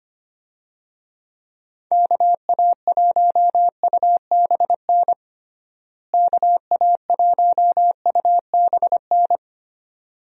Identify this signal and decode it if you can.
Morse code, heard as a keyed tone, about 25 wpm, 700 Hz. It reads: KA1UBN KA1UBN
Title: USA Call Signs - 6 Characters - Repeated 2x - 25wpm